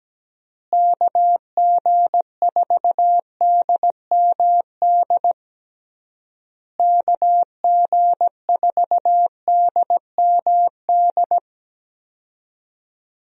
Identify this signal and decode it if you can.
Morse code, heard as a keyed tone, about 17 wpm, 700 Hz. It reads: KG4DMD KG4DMD
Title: USA Call Signs - 6 Characters - Repeated 3x - 17wpm